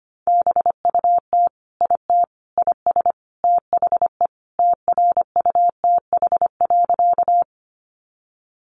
Morse code, heard as a keyed tone, about 25 wpm, 700 Hz. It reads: BUT IT IS THE TRUTH.
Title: Sentences from Top 800 Words 25wpm